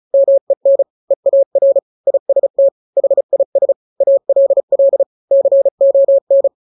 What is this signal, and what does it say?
Morse code, heard as a keyed tone, about 34 wpm, 550 Hz. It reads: MEN EAR IST HIS ALL CON
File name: 3-grams 35 30 WPM